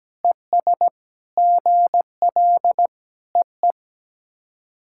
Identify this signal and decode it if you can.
Morse code, heard as a keyed tone, about 17 wpm, 700 Hz. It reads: ES GL EE
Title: SOTA QSO Practice 17wpm